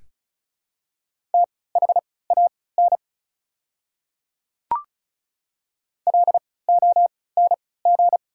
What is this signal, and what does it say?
Morse code, heard as a keyed tone, about 36 wpm, spaced as for 20 wpm, 690 Hz. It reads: THAN LONG
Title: 3x Character Spacing - Top 100 Words - 35wpm